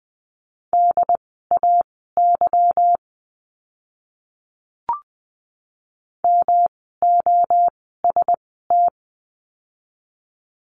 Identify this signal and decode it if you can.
Morse code, heard as a keyed tone, about 20 wpm, spaced as for 14 wpm, 700 Hz. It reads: DAY MOST
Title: 244 Lesson 39f - Letter z - Words 20wpm